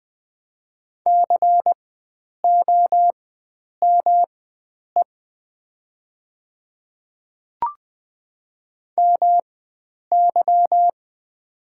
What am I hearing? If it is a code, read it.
Morse code, heard as a keyed tone, about 20 wpm, spaced as for 9 wpm, 700 Hz. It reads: COME MY
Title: 4x Character Spacing - Top 100 Words - 20wpm